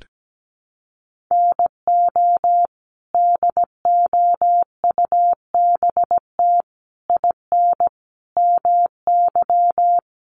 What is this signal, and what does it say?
Morse code, heard as a keyed tone, about 17 wpm, 700 Hz. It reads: NO DOUBT IN MY
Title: Sets of 5 Words Encore2 17wpm